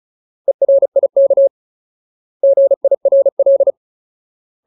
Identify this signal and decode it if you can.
Morse code, heard as a keyed tone, about 35 wpm, 550 Hz. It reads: ERIK GIRL
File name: Ultimate Word List / 35WPM Ultimate